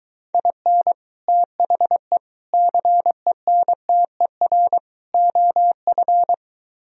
Morse code, heard as a keyed tone, about 23 wpm, 700 Hz. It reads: IN THE CENTER OF